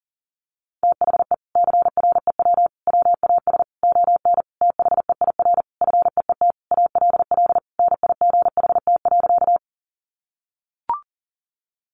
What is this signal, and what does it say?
Morse code, heard as a keyed tone, about 40 wpm, 700 Hz. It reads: THE CREW WAS ON THEIR FEET ALL NIGHT.
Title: Sentences from Top 1500 Words 40wpm